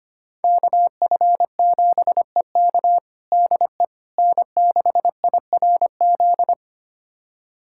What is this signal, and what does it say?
Morse code, heard as a keyed tone, about 25 wpm, 700 Hz. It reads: KF7EK DE N6IRZ
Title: USA Callsign DE Callsign - 5 Characters - 25wpm